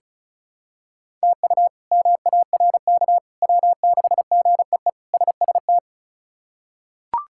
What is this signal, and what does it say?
Morse code, heard as a keyed tone, about 34 wpm, 700 Hz. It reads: TU MARK W6GEE SST
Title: K1USN SST QSO Practice - 35wpm